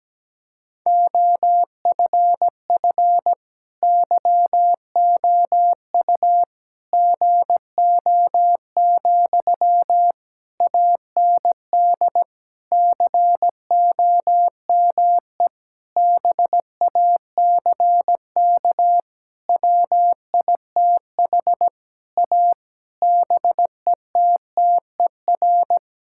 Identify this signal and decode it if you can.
Morse code, heard as a keyed tone, about 17 wpm, 700 Hz. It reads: OFF YOU GO, AND COME BACK WITH A BETTER
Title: Sentences from Top 1000 Words - Review Part 6 - 17wpm